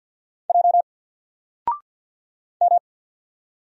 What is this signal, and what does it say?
Morse code, heard as a keyed tone, about 50 wpm, 700 Hz. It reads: J M
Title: Single Letter-Number 50wpm